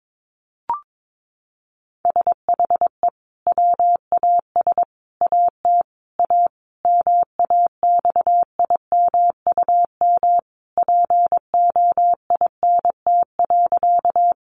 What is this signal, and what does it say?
Morse code, heard as a keyed tone, about 22 wpm, 700 Hz. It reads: SHE WAS AT A MAXIMUM POINT.
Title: Sentences from Top 1300 Words 22wpm